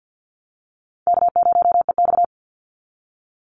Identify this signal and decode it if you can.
Morse code, heard as a keyed tone, about 51 wpm, 700 Hz. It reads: K0EX
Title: Call Signs Repeated 2x - 50wpm